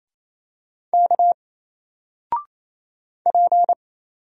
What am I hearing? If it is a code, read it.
Morse code, heard as a keyed tone, about 28 wpm, 700 Hz. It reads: K P
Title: Single Letter-Number 28wpm